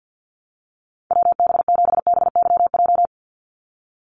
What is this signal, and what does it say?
Morse code, heard as a keyed tone, about 50 wpm, 700 Hz. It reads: WB7BYJ
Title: USA Call Signs 50wpm